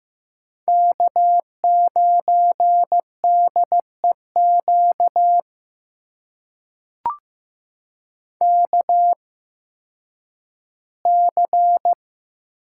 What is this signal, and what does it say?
Morse code, heard as a keyed tone, about 15 wpm, spaced as for 10 wpm, 700 Hz. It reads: K9DEQ KC
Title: USA Call Signs - 8x Character Spacing - 15wpm